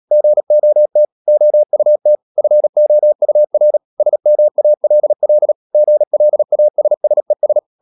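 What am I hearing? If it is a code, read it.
Morse code, heard as a keyed tone, about 37 wpm, 600 Hz. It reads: GOT OUT FOUR SMALL GLASSES
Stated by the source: Morse code practice at 35–39 wpm sending English text of Ken Follett's novel Never, from CW med Gnister.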